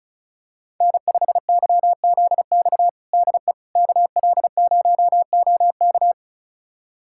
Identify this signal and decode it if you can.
Morse code, heard as a keyed tone, about 34 wpm, 700 Hz. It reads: N5YZX DE KL0OK
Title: USA Callsign DE Callsign - 5 Characters - 35wpm